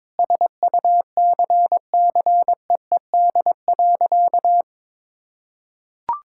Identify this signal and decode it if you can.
Morse code, heard as a keyed tone, about 22 wpm, 700 Hz. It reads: SUCCEED.
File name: practice / 22wpm / Sentences with 2-Word Phrases - 22wpm